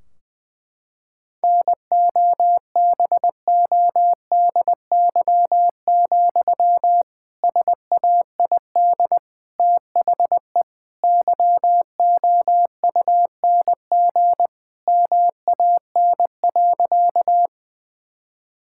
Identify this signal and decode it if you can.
Morse code, heard as a keyed tone, about 20 wpm, 700 Hz. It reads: NOBODY, SAID THE YOUNG MAN.
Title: Sentences from Top 1600 Words 20wpm